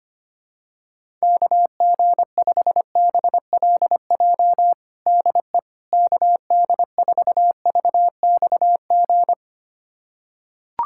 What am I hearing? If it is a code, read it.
Morse code, heard as a keyed tone, about 25 wpm, 700 Hz. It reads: KG5BLJ DE KD4VXG
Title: USA Callsign DE Callsign 25wpm